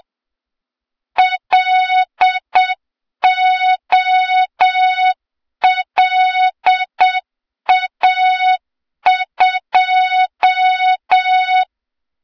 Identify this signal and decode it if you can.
Morse code, heard as a keyed tone, about 7 wpm, 3700 Hz, with noise. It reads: LOLA2
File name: Morze-HA2MN / hang / 23 GYAKORLAT